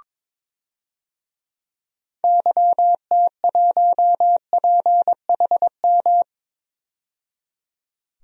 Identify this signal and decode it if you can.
Morse code, heard as a keyed tone, about 21 wpm, 700 Hz. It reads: YT1PHM
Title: DXCC-100 Call Signs - 6 Characters - 22wpm